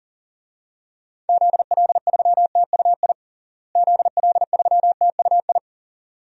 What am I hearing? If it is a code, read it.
Morse code, heard as a keyed tone, about 37 wpm, 700 Hz. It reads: ZL3TUI ZL3TUI
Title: DXCC-100 Call Signs - 6 Characters - Repeated 2x - 40wpm